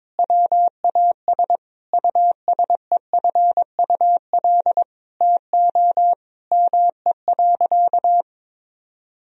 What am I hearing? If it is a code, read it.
Morse code, heard as a keyed tone, about 22 wpm, 700 Hz. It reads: WAS USEFUL TO ME.
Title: Sentences from Top 1000 Words - Review Part 4 - 22wpm